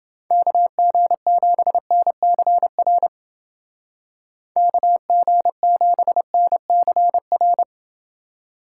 Morse code, heard as a keyed tone, about 28 wpm, 700 Hz. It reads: KG7NCR KG7NCR